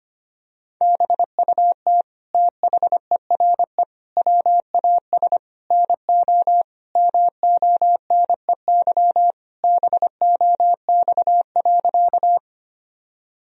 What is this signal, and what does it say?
Morse code, heard as a keyed tone, about 25 wpm, 700 Hz. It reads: BUT THERE WAS NO MONEY BOX.
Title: Sentences from Top 1000 Words - Review Part 2 - 25wpm